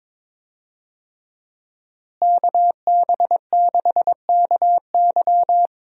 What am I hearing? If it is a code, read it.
Morse code, heard as a keyed tone, about 22 wpm, 700 Hz. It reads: KB6KY
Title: USA Call Signs - 5 Characters - Repeated 3x - 22wpm